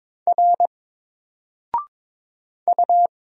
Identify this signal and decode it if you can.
Morse code, heard as a keyed tone, about 22 wpm, 700 Hz. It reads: R U
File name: Single-Letter-Number-22wpm